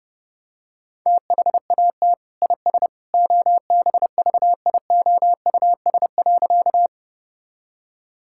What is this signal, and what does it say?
Morse code, heard as a keyed tone, about 30 wpm, 700 Hz. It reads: THAT IS OBVIOUS.
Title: Sentences from Top 1600 Words 30wpm